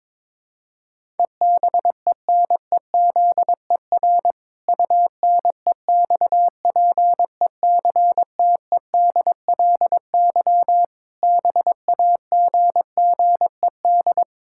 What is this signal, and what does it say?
Morse code, heard as a keyed tone, about 22 wpm, 700 Hz. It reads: EBENEZER UNEXPECTEDLY BAGGED
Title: Pangram Challenge v2 - 22wpm